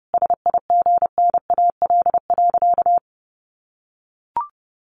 Morse code, heard as a keyed tone, about 30 wpm, 700 Hz. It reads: SIGNAL.
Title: Sentences from Top 1400 Words 30wpm